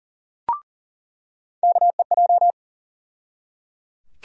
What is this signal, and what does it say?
Morse code, heard as a keyed tone, about 40 wpm, 700 Hz. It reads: KEJ
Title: USA Call Sign Postfixes - 3 Letters 40wpm